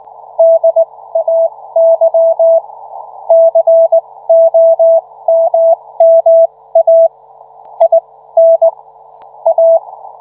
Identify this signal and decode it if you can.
Morse code, heard as a keyed tone, about 19 wpm, spaced as for 15 wpm, 660 Hz. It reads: DAY COMMA IN A